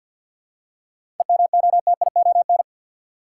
Morse code, heard as a keyed tone, about 52 wpm, 700 Hz. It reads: EMOTION
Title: Seven Letter Words - 50wpm